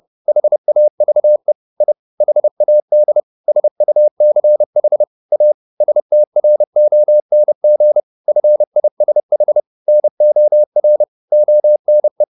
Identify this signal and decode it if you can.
Morse code, heard as a keyed tone, about 30 wpm, 600 Hz. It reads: HAVE I HAD SUCH A STRONG FISH NOR ONE
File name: the old man and the sea / the old man and the sea 1 30wpm